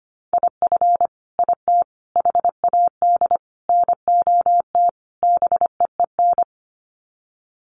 Morse code, heard as a keyed tone, about 25 wpm, 700 Hz. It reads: IF IT HAD NOT BEEN
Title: Sets of 5 Words Encore 25wpm